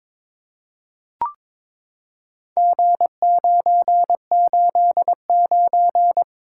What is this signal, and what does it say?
Morse code, heard as a keyed tone, about 22 wpm, 700 Hz. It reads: G989
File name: practice / 22wpm / Four Letter-Number Combinations - 22wpm